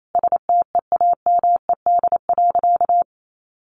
Morse code, heard as a keyed tone, about 28 wpm, 700 Hz. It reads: STEAMED.
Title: A Sound Of Thunder 28wpm